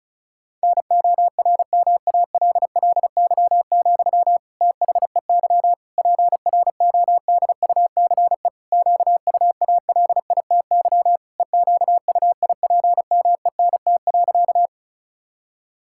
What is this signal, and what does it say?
Morse code, heard as a keyed tone, about 36 wpm, 690 Hz. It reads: NORMALLY, THEY PRODUCE QUALITY EQUIPMENT.
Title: Sentences from Top 1000 Words - Review Part 6 - 35wpm